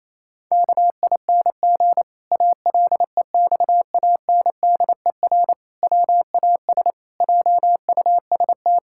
Morse code, heard as a keyed tone, about 28 wpm, 700 Hz. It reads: KING ALEXANDER WAS JUST
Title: Pangram Challenge v2 - 28wpm